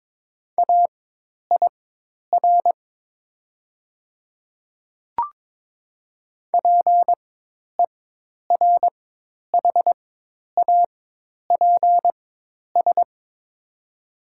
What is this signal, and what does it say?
Morse code, heard as a keyed tone, about 22 wpm, spaced as for 10 wpm, 700 Hz. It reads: AIR PERHAPS
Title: 4x Character Spacing - Top 500 Words - 22wpm